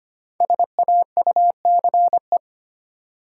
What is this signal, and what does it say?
Morse code, heard as a keyed tone, about 25 wpm, 700 Hz. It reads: SAUCE
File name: 5 Letter Words - First 3 Letters Given - 25wpm